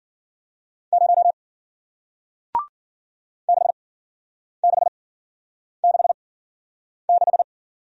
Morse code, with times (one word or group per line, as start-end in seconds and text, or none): 0.9-1.3 0
3.5-7.4 6666